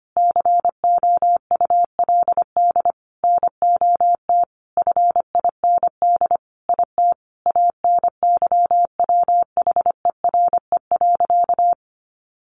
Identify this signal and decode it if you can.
Morse code, heard as a keyed tone, about 25 wpm, 700 Hz. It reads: COULD NOT FIND IT ANYWHERE.
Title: Sentences from Top 1400 Words 25wpm